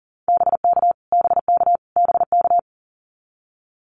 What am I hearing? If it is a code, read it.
Morse code, heard as a keyed tone, about 40 wpm, 700 Hz. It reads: BK BK BK
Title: Abbreviations Repeated 3x - 40wpm